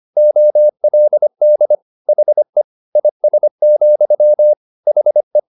Morse code, heard as a keyed tone, about 25 wpm, 600 Hz. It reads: OLD HE IS, HE
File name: the old man and the sea part 1 25wpm